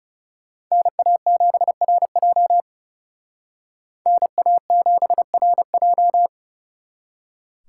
Morse code, heard as a keyed tone, about 38 wpm, 700 Hz. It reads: NA7RJ NA7RJ